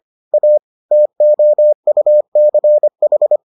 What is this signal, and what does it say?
Morse code, heard as a keyed tone, about 25 wpm, 600 Hz. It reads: A TOUCH